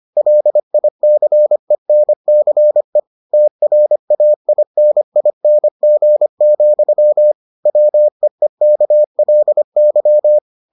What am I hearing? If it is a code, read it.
Morse code, heard as a keyed tone, about 25 wpm, 600 Hz. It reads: LICENCE TRAINING, WEEKLY